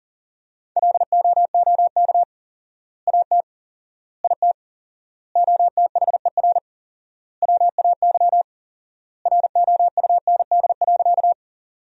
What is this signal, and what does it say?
Morse code, heard as a keyed tone, about 40 wpm, 700 Hz. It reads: LOOK AT IT OTHER WAY ROUND.
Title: 4x Word-Spacing - Sentences from Top 200 Words - 40wpm